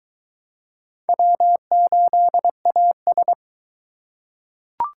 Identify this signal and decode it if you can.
Morse code, heard as a keyed tone, about 21 wpm, 700 Hz. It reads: W8AS